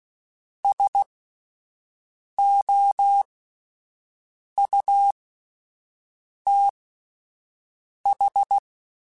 Morse code, heard as a keyed tone, about 16 wpm, spaced as for 5 wpm, 800 Hz. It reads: SOUTH
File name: QSOs Spaced